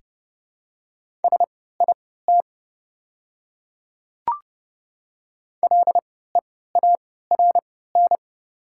Morse code, heard as a keyed tone, about 30 wpm, spaced as for 17 wpm, 700 Hz. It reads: SIT LEARN
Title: 3x Character Spacing - Top 500 Words - 30wpm